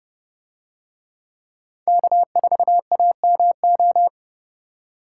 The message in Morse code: K4AMO

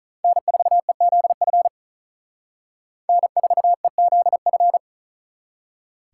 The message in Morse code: N4EZF N4EZF

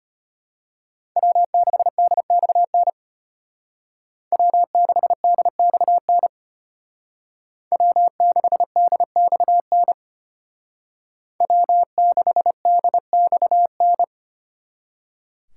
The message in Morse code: W6DXN W6DXN W6DXN W6DXN

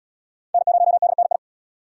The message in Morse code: N0DNI

